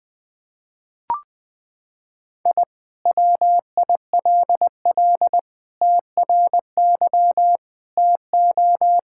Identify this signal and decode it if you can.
Morse code, heard as a keyed tone, about 20 wpm, 700 Hz. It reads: I WILL TRY TO
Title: Sentences with 3-Word Phrases - 20wpm